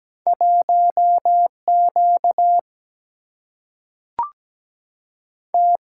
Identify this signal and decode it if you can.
Morse code, heard as a keyed tone, about 17 wpm, 700 Hz. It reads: 1 Q T
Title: Single-Letter Single-Number 17wpm